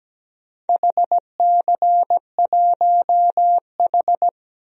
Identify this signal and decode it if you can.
Morse code, heard as a keyed tone, about 17 wpm, 700 Hz. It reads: HC1H
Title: DXCC-100 Call Signs - 4 Characters - Repeated 2x - 17wpm